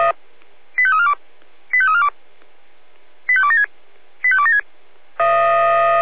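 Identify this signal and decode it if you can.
Five-tone selective call (ZVEI), heard as quick runs of five tones, tones of about 70 ms. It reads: 85342 85342 85276 85276
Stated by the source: ZVEI five-tone selective call received by SDR.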